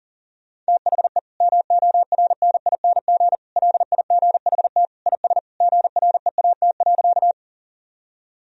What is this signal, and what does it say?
Morse code, heard as a keyed tone, about 40 wpm, 700 Hz. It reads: THE MORNING LIGHT IS GREAT.
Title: Sentences from Top 1000 Words - Review Part 1 - 40wpm